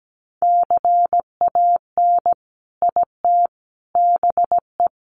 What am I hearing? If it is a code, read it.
Morse code, heard as a keyed tone, about 17 wpm, 700 Hz. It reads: CAN IT BE
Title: Sentences from Top 600 Words 17wpm